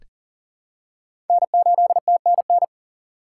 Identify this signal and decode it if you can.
Morse code, heard as a keyed tone, about 42 wpm, 700 Hz. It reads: N8TNN
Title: USA Call Signs - 5 Characters - 6x Character Spacing - 40wpm